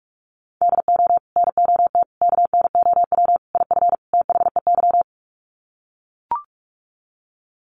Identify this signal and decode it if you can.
Morse code, heard as a keyed tone, about 44 wpm, 700 Hz. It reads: DO NOT KNOW IF THEY
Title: Sets of 5 Words Encore 45wpm